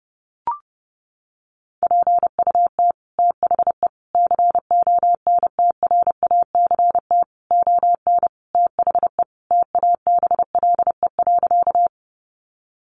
Morse code, heard as a keyed tone, about 30 wpm, 700 Hz. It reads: PUT THE CONTRACT ON THE TABLE.
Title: Sentences from Top 1400 Words 30wpm